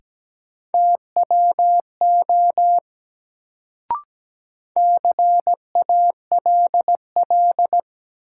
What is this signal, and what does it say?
Morse code, heard as a keyed tone, about 17 wpm, 700 Hz. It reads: TWO CALL
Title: Sentences from Top 100 Words - New Word Drill - 17wpm